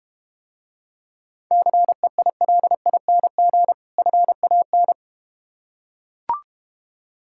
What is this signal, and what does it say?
Morse code, heard as a keyed tone, about 32 wpm, 700 Hz. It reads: CEILING FAN